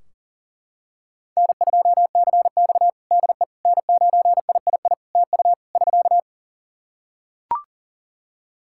Morse code, heard as a keyed tone, about 37 wpm, 700 Hz. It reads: N2CX DE N9III TU <SK>
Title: K1USN SST QSO Practice 40wpm